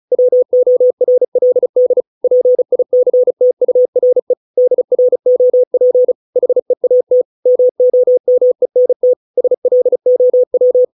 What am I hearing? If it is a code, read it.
Morse code, heard as a keyed tone, about 35 wpm, 500 Hz. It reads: WORLD PICTURE DROP HEAT MOMENT SLOW